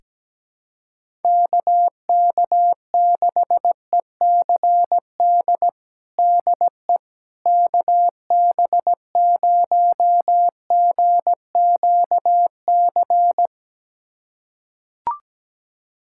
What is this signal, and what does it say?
Morse code, heard as a keyed tone, about 17 wpm, 700 Hz. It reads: KK6ECD DE KB0GQC
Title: USA Callsign DE Callsign - 6 Characters - 17wpm